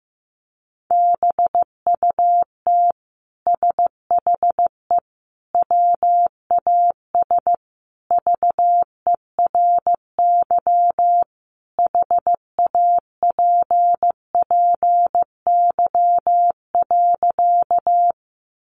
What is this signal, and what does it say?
Morse code, heard as a keyed tone, about 15 wpm, 700 Hz. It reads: BUT SHE WAS VERY HAPPY.
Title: Sentences from Top 1000 Words 15wpm